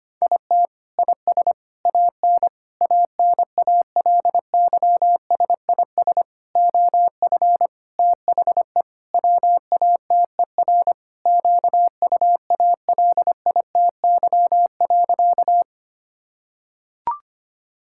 Morse code, heard as a keyed tone, about 25 wpm, 700 Hz. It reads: IT IS AN ANALYSIS OF THE WATER QUALITY.
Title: Sentences from Top 1000 Words - Review Part 5 - 25wpm